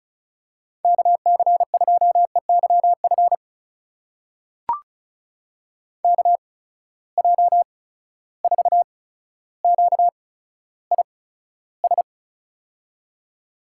Morse code, heard as a keyed tone, about 36 wpm, spaced as for 15 wpm, 700 Hz. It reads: KC2EYF KJ4QIS
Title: USA Call Signs - 6 Characters - 8x Character Spacing - 35wpm